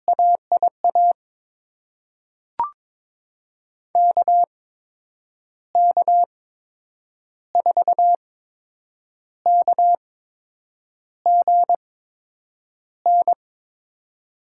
Morse code, heard as a keyed tone, about 22 wpm, spaced as for 8 wpm, 700 Hz. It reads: AIA KK4KGN
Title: USA Call Signs - 6 Characters - 8x Character Spacing - 22wpm